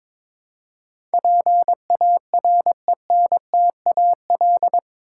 Morse code, heard as a keyed tone, about 22 wpm, 700 Hz. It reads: PARENTAL
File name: practice / 22wpm / Eight Letter Words - 22wpm